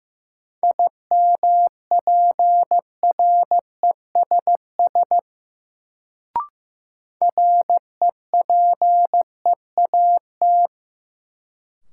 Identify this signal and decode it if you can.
Morse code, heard as a keyed tone, about 15 wpm, 700 Hz. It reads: IMPRESS REPEAT
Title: Sentences from Top 1600 Words - New Word Drill - 15wpm